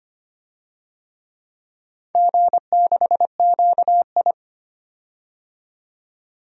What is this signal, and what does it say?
Morse code, heard as a keyed tone, about 25 wpm, 700 Hz. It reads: G6QI
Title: DXCC-100 Call Signs - 4 Characters - 25wpm